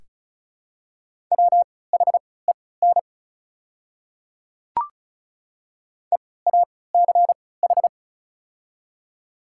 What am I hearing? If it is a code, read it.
Morse code, heard as a keyed tone, about 36 wpm, spaced as for 20 wpm, 700 Hz. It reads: WHEN EACH
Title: 3x Character Spacing - Top 200 Words - 35wpm